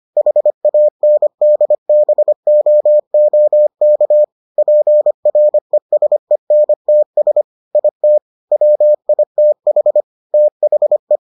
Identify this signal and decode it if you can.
Morse code, heard as a keyed tone, about 25 wpm, 600 Hz. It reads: HANDBOOK PRESENTS IT WITH THE